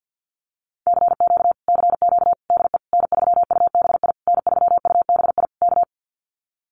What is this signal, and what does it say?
Morse code, heard as a keyed tone, about 50 wpm, 700 Hz. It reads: CQ CQ DE N3UBI N3UBI K
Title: Common QSO Phrases Encore 50wpm